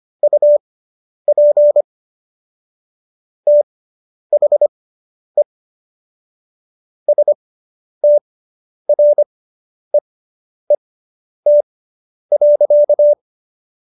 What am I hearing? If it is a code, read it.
Morse code, heard as a keyed tone, about 25 wpm, spaced as for 10 wpm, 590 Hz. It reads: UP THE STREET.